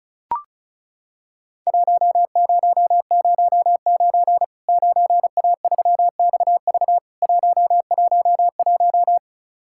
Morse code, heard as a keyed tone, about 34 wpm, 700 Hz. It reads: 1009 9A3XV 111
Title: Worked All Europe DX Contest - QTC Exchange - 35wpm